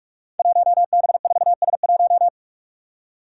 Morse code, heard as a keyed tone, about 46 wpm, 700 Hz. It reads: 1B4S1
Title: Five Challenging Letter-Number Combinations 45wpm